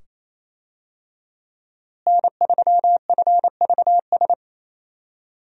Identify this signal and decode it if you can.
Morse code, heard as a keyed tone, about 28 wpm, 700 Hz. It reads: N3FVS
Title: USA Call Signs - 5 Characters - Repeated 2x - 28wpm